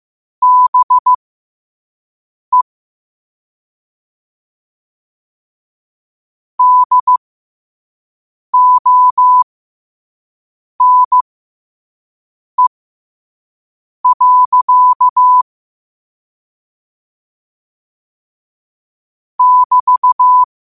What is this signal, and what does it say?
Morse code, heard as a keyed tone, about 15 wpm, spaced as for 5 wpm, 1000 Hz. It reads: BE DONE. <BT>
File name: qotdfiles / 5wpm